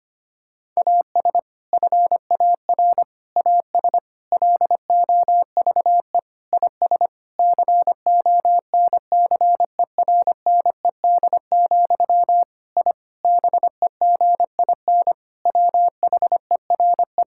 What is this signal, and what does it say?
Morse code, heard as a keyed tone, about 25 wpm, 700 Hz. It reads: AS FAR AS LOVE IS CONCERNED, I BEGIN WHERE
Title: Sentences from Top 1000 Words - Review Part 6 - 25wpm